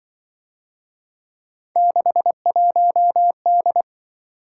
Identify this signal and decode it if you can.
Morse code, heard as a keyed tone, about 24 wpm, 700 Hz. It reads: 61D
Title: Speed-Racing - Three Letter-Number Combinations - 20wpm